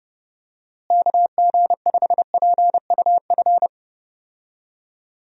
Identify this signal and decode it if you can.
Morse code, heard as a keyed tone, about 30 wpm, 700 Hz. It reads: KG5PUF
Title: USA Call Signs - 6 Characters - Repeated 2x - 30wpm